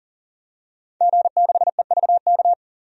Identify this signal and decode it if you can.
Morse code, heard as a keyed tone, about 40 wpm, 700 Hz. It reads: G6EVK
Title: DXCC-100 Call Signs - 5 Characters - 40wpm